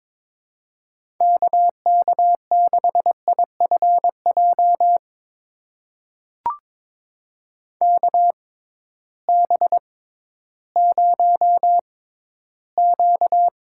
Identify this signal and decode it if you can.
Morse code, heard as a keyed tone, about 21 wpm, spaced as for 13 wpm, 700 Hz. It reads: KK6IFJ KB0Q
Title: USA Call Signs - 6x Character Spacing - 22wpm